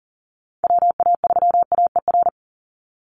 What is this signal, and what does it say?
Morse code, heard as a keyed tone, about 40 wpm, 700 Hz. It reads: WA3AER
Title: Call Signs - 40wpm